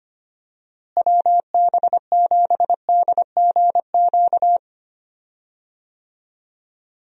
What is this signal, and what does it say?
Morse code, heard as a keyed tone, about 25 wpm, 700 Hz. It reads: WB7DGQ
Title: USA Call Signs - 6 Characters - Repeated 2x - 25wpm